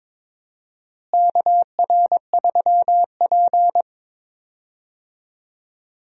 Morse code, heard as a keyed tone, about 22 wpm, 700 Hz. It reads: KR3P
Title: USA Call Signs - 4 Characters - Repeated 2x - 22wpm